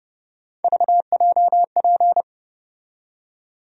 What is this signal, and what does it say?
Morse code, heard as a keyed tone, about 30 wpm, 700 Hz. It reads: VJP
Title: USA Call Sign Postfixes - 3 Letters 30wpm